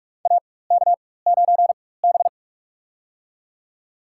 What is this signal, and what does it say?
Morse code, heard as a keyed tone, about 46 wpm, spaced as for 21 wpm, 700 Hz. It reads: AK9B
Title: USA Call Signs - 4 Characters - 4x Character Spacing - 45wpm